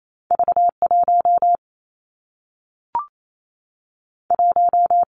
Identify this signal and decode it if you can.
Morse code, heard as a keyed tone, about 28 wpm, 700 Hz. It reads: V 1 1
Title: Two Challenging Letter-Number Combinations 28wpm